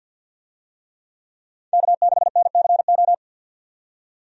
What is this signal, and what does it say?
Morse code, heard as a keyed tone, about 48 wpm, 700 Hz. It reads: K6NCO
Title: USA Call Signs - 5 Characters - 50wpm